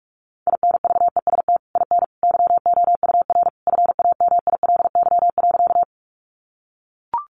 Morse code, heard as a keyed tone, about 44 wpm, 700 Hz. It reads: INVEST IN YOUR FAMILY.
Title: Sentences from Top 1100 Words 45wpm